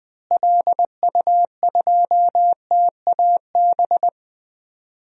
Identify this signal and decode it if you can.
Morse code, heard as a keyed tone, about 21 wpm, 700 Hz. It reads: LU2TAB
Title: DXCC-100 Call Signs - 6 Characters - Repeated 2x - 20wpm